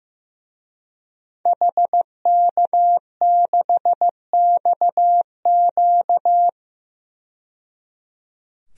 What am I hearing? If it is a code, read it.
Morse code, heard as a keyed tone, about 15 wpm, 700 Hz. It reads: HK6XQ
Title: DXCC-100 Call Signs - 5 Characters - 15wpm